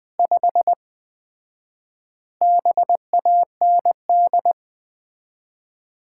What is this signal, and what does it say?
Morse code, heard as a keyed tone, about 20 wpm, 700 Hz. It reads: 5 BAND
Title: Common QSO Phrases - 4x Word-Spacing - 20wpm